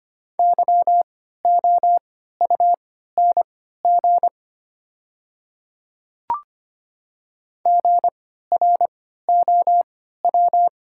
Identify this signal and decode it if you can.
Morse code, heard as a keyed tone, about 25 wpm, spaced as for 14 wpm, 690 Hz. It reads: YOUNG GROW
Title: 3x Character Spacing - Top 500 Words - 25wpm